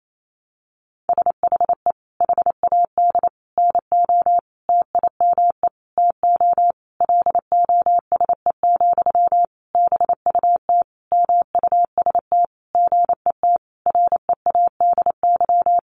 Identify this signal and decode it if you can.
Morse code, heard as a keyed tone, about 28 wpm, 700 Hz. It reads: SHE HAD NO TIME TO LOSE, BUT MUST GET READY
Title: Sentences from Top 1100 Words 28wpm